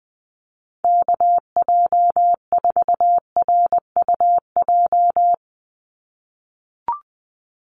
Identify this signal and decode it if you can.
Morse code, heard as a keyed tone, about 20 wpm, 700 Hz. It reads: KJ4RUJ
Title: USA Call Signs 20wpm